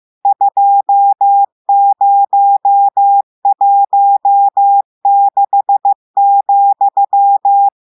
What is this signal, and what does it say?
Morse code, heard as a keyed tone, about 15 wpm, 800 Hz. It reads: 2016,
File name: eh1ard-amateur-radio-day-2016 mf feed 1